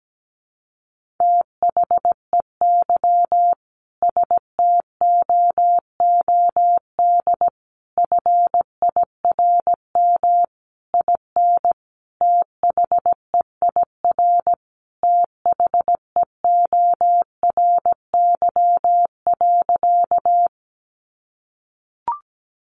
Morse code, heard as a keyed tone, about 17 wpm, 700 Hz. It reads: THEY STOOD FIRM IN THEIR THEORY.
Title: Sentences from Top 1400 Words 17wpm